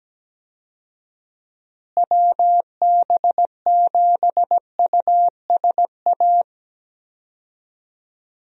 Morse code, heard as a keyed tone, about 17 wpm, 700 Hz. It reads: WB7USA
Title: USA Call Signs - 6 Characters - Repeated 3x - 17wpm